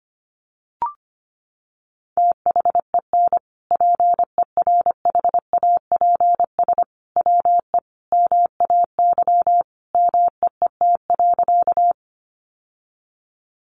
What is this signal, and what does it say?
Morse code, heard as a keyed tone, about 25 wpm, 700 Hz. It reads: THEN PERHAPS WE MAY MEET.
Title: Sentences from Top 1100 Words 25wpm